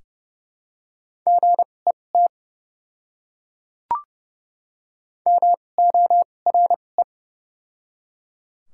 Morse code, heard as a keyed tone, about 30 wpm, spaced as for 22 wpm, 700 Hz. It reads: GET MORE
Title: 2x Character Spacing - Top 200 Words - 30wpm